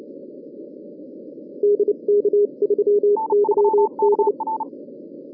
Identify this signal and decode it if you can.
Morse code, heard as a keyed tone, about 29 wpm, 410 Hz. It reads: DK3YD